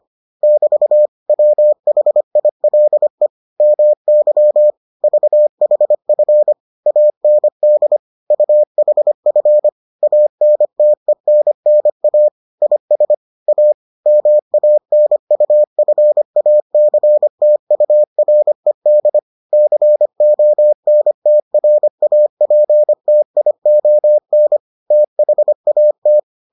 Morse code, heard as a keyed tone, about 25 wpm, 600 Hz. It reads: <BT> WHILE MY VHF AND UHF ANTENNA IS A MANUFACTURED CONTRAPTION THAT